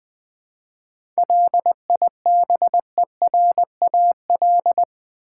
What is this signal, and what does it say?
Morse code, heard as a keyed tone, about 20 wpm, 700 Hz. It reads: LIBERAL
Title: 7 Letter Words - First 3 Letters Given - 20wpm